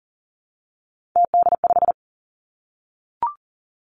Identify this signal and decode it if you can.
Morse code, heard as a keyed tone, about 38 wpm, 700 Hz. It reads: TD5
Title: Three Challenging Letter-Number Combinations 40wpm